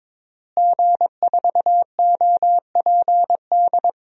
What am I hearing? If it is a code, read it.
Morse code, heard as a keyed tone, about 22 wpm, 700 Hz. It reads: G4OPD
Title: DXCC-100 Call Signs - 5 Characters - Repeated 3x - 22wpm